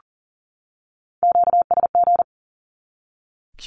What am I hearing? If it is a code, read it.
Morse code, heard as a keyed tone, about 40 wpm, 700 Hz. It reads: QSG
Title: Q-Codes 40wpm